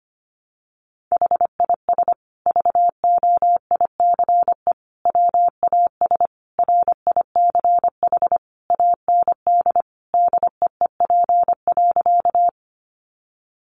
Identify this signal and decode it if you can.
Morse code, heard as a keyed tone, about 25 wpm, 700 Hz. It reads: HIS VOICE WAS RICH AND DEEP.
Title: Sentences from Top 1100 Words 25wpm